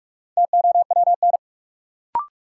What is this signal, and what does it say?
Morse code, heard as a keyed tone, about 47 wpm, 700 Hz. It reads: TOWN
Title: Sentences from Top 500 Words - New Word Drill - 45wpm